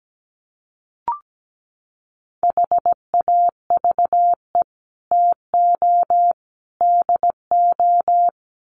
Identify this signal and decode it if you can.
Morse code, heard as a keyed tone, about 17 wpm, 700 Hz. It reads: HAVE TO DO
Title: Sets of 3 Words 17wpm